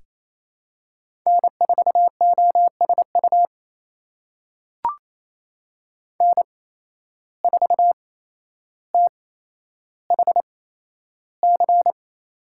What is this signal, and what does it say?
Morse code, heard as a keyed tone, about 28 wpm, spaced as for 12 wpm, 700 Hz. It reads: N4OSU N4THC
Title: USA Call Signs - 5 Characters - 8x Character Spacing - 28wpm